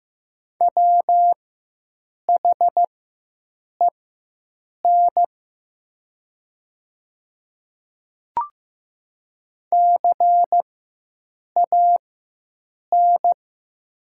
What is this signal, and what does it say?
Morse code, heard as a keyed tone, about 15 wpm, spaced as for 7 wpm, 700 Hz. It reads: WHEN CAN
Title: 4x Character Spacing - Top 100 Words - 15wpm